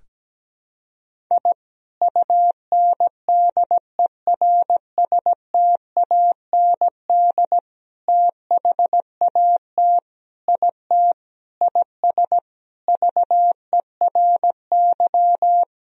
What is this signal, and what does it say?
Morse code, heard as a keyed tone, about 17 wpm, 700 Hz. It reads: I UNDERSTAND THAT IT IS VERY
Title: Sentences from Top 1000 Words - Review Part 1 - 17wpm